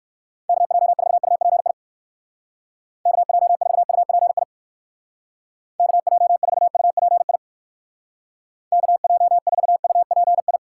Morse code, heard as a keyed tone, about 59 wpm, 700 Hz. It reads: KJ4UPI KJ4UPI KJ4UPI KJ4UPI